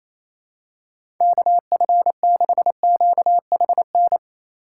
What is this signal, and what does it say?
Morse code, heard as a keyed tone, about 27 wpm, 700 Hz. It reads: KF6QHN
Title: USA Call Signs - 6 Characters - 3x Character Spacing - 28wpm